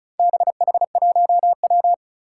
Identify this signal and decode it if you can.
Morse code, heard as a keyed tone, about 36 wpm, 700 Hz. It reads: BH1W
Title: Four Challenging Letter-Number Combinations 35wpm